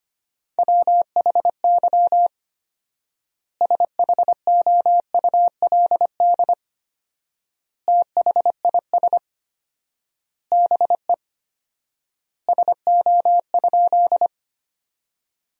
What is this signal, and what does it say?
Morse code, heard as a keyed tone, about 25 wpm, 700 Hz. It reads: WHY SHOULD THIS BE SO?
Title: 4x Word-Spacing - Sentences from Top 300 Words - 25wpm